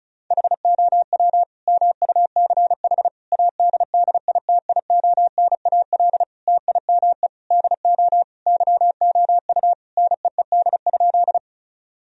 Morse code, heard as a keyed tone, about 36 wpm, 700 Hz. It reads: HOW MUCH ADDITIONAL TIME DO YOU NEED?
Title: Sentences from Top 1000 Words - Review Part 4 - 35wpm